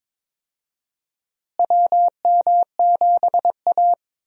Morse code, heard as a keyed tone, about 22 wpm, 700 Hz. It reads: WM7A
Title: USA Call Signs - 4 Characters - Repeated 3x - 22wpm